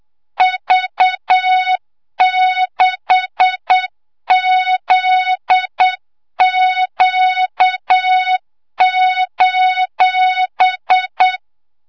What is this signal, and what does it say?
Morse code, heard as a keyed tone, about 8 wpm, 3700 Hz, with noise. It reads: V6ZQ:
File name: Morze-HA2MN / hangp / 2 06 GYAKORLAT